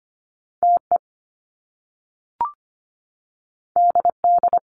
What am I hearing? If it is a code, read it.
Morse code, heard as a keyed tone, about 25 wpm, 700 Hz. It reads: TE DD
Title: Top Two Letter Combinations in Words 25wpm